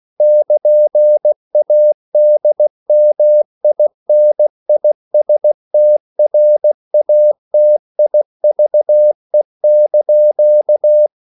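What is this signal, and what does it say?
Morse code, heard as a keyed tone, about 16 wpm, 600 Hz. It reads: <KN>ADMINISTRATIVE)
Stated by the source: CW practice of national capitals at 16 words per minute, from ON6ZQ.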